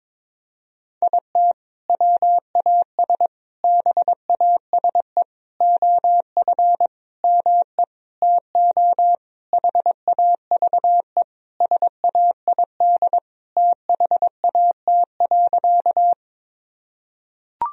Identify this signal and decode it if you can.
Morse code, heard as a keyed tone, about 22 wpm, 700 Hz. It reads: IT WAS BASE OF ME TO HAVE SAID THAT.
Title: Sentences from Top 1000 Words - Review Part 1 - 22wpm